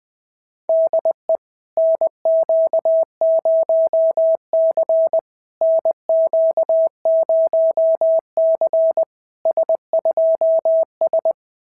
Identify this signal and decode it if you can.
Morse code, heard as a keyed tone, about 21 wpm, 660 Hz. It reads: DE NQ0C NQ0C S2S
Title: SOTA QSO Practice 20wpm